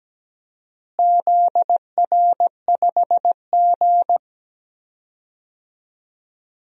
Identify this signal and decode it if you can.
Morse code, heard as a keyed tone, about 17 wpm, 700 Hz. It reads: ZR5G
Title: DXCC-100 Call Signs - 4 Characters - Repeated 2x - 17wpm